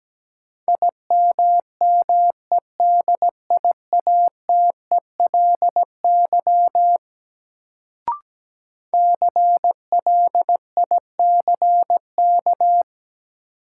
Morse code, heard as a keyed tone, about 17 wpm, 700 Hz. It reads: IMMEDIATELY CLICK
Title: Sentences from Top 1100 Words - New Word Drill - 17wpm